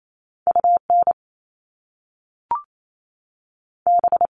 Morse code, heard as a keyed tone, about 28 wpm, 700 Hz. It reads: U N B
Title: Top Two Letter Combinations in Words 28wpm